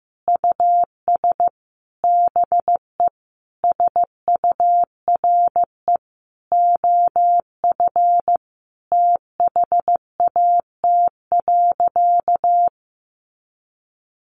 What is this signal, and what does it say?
Morse code, heard as a keyed tone, about 14 wpm, 700 Hz. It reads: US BE SURE OF THAT.
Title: Sentences from Top 300 Words 15wpm